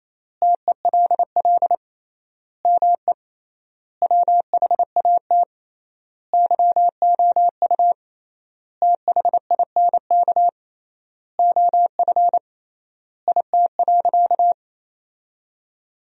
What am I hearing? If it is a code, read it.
Morse code, heard as a keyed tone, about 28 wpm, 700 Hz. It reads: TELL ME WHAT YOU THINK OF IT.
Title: 3x Word-Spacing - Sentences from Top 200 Words - 28wpm